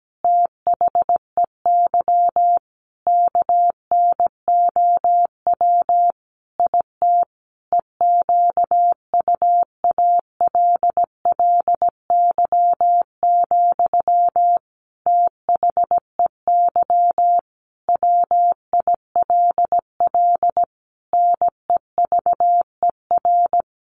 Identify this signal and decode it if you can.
Morse code, heard as a keyed tone, about 17 wpm, 700 Hz. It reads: THEY KNOW IT EQUALLY, THEY WILL NEVER
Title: Sentences from Top 1400 Words 17wpm